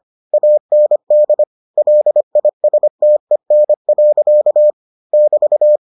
Morse code, heard as a keyed tone, about 25 wpm, 600 Hz. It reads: AND LISTEN. <BT>